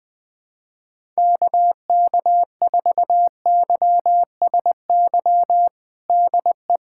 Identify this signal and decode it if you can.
Morse code, heard as a keyed tone, about 20 wpm, 700 Hz. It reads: KK4YSY DE
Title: USA Callsign DE Callsign 20wpm